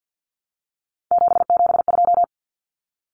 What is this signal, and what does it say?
Morse code, heard as a keyed tone, about 48 wpm, 700 Hz. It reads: 772
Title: Numbers - 3-Digit 50wpm